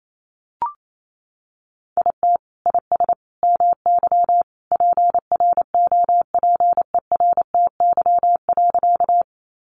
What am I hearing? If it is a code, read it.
Morse code, heard as a keyed tone, about 28 wpm, 700 Hz. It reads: IT IS MY PROPERTY.
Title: Sentences from Top 1000 Words 28wpm